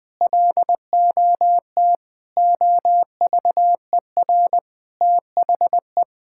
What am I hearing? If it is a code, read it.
Morse code, heard as a keyed tone, about 20 wpm, 700 Hz. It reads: LOT OVER THE
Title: Sentences with 3-Word Phrases - 20wpm